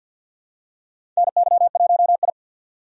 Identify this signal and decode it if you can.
Morse code, heard as a keyed tone, about 53 wpm, 700 Hz. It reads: NY1I
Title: USA Call Signs - 4 Characters - 50wpm